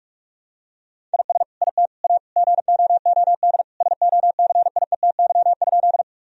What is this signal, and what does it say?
Morse code, heard as a keyed tone, about 46 wpm, 700 Hz. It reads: IS IT A GOOD SOCIETY?
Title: Sentences from Top 1000 Words - Review Part 4 - 45wpm